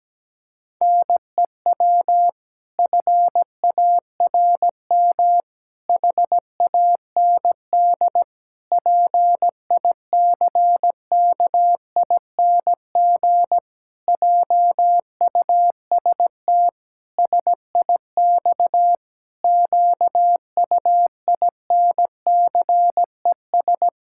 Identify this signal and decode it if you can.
Morse code, heard as a keyed tone, about 17 wpm, 700 Hz. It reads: NEW FARM HAND PICKING JUST SIX QUINCES
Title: Pangram Challenge v2 - 17wpm